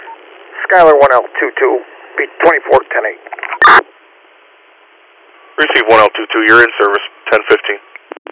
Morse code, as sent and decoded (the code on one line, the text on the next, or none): none